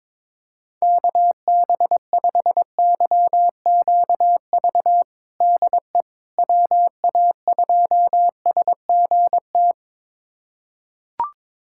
Morse code, heard as a keyed tone, about 22 wpm, 700 Hz. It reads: KB5YQV DE WA2SGT
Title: USA Callsign DE Callsign 22wpm